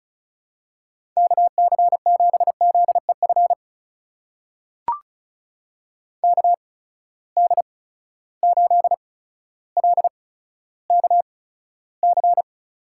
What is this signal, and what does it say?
Morse code, heard as a keyed tone, about 36 wpm, spaced as for 15 wpm, 700 Hz. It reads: KC7ZEF KD8LKC
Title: USA Call Signs - 6 Characters - 8x Character Spacing - 35wpm